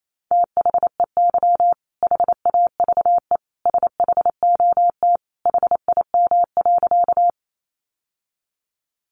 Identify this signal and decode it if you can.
Morse code, heard as a keyed tone, about 28 wpm, 700 Hz. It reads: THEY HAVE SHOT HIM.
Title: Sentences from Top 1100 Words 28wpm